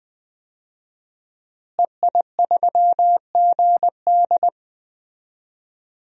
Morse code, heard as a keyed tone, about 20 wpm, 700 Hz. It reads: EI3GD
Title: DXCC-100 Call Signs - 5 Characters - Repeated 2x - 20wpm